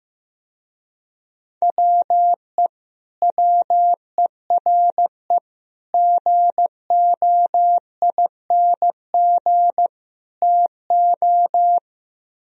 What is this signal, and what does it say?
Morse code, heard as a keyed tone, about 15 wpm, 700 Hz. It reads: WE WERE GOING TO
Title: Speed-Racing - Sets of 4 Words 15wpm